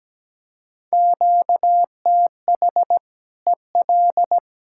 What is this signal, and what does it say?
Morse code, heard as a keyed tone, about 17 wpm, 700 Hz. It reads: QTH EL
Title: Practice QSO - Exchange 1 - Beginning - 17wpm